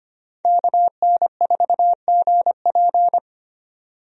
Morse code, heard as a keyed tone, about 25 wpm, 700 Hz. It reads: KN4GP
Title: USA Call Signs - 5 Characters - 25wpm